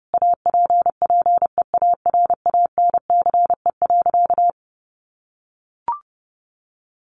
Morse code, heard as a keyed tone, about 30 wpm, 700 Hz. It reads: APPEARANCE.
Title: Sentences from Top 1100 Words 30wpm